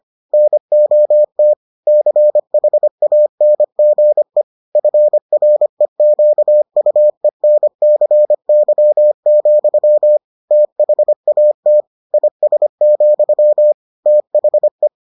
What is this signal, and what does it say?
Morse code, heard as a keyed tone, about 25 wpm, 600 Hz. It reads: NOT CHANGE FREQUENCY, THAT IS, THE